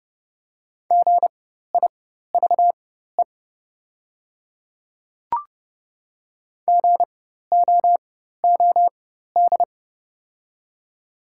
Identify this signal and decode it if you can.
Morse code, heard as a keyed tone, about 31 wpm, spaced as for 14 wpm, 700 Hz. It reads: GIVE GOOD
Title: 4x Character Spacing - Top 300 Words - 30wpm